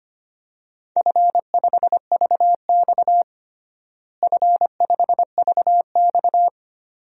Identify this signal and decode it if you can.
Morse code, heard as a keyed tone, about 25 wpm, 700 Hz. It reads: F5VX F5VX
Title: DXCC-100 Call Signs - 4 Characters - Repeated 2x - 25wpm